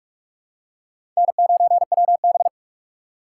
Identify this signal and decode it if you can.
Morse code, heard as a keyed tone, about 46 wpm, 700 Hz. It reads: N9WB
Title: USA Call Signs - 4 Characters - 2x Character Spacing - 45wpm